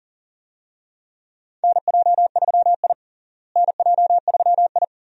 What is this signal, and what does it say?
Morse code, heard as a keyed tone, about 40 wpm, 700 Hz. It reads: NJ3I NJ3I
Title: USA Call Signs - 4 Characters - Repeated 2x - 40wpm